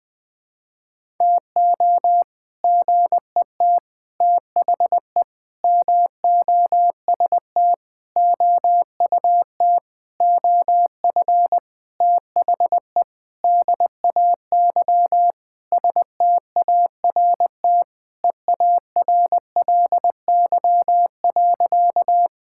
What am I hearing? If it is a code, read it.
Morse code, heard as a keyed tone, about 20 wpm, 700 Hz. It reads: TO GET THE MOST OUT OF THE DAY START EARLY.
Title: Sentences with 7-Word Phrases - 20wpm